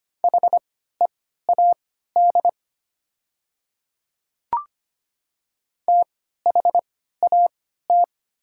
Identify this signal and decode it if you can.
Morse code, heard as a keyed tone, about 25 wpm, spaced as for 14 wpm, 700 Hz. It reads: HEAD THAT
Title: 3x Character Spacing - Top 300 Words - 25wpm